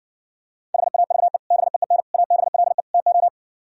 Morse code, heard as a keyed tone, about 64 wpm, 700 Hz. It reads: HAVE BEEN ABLE TO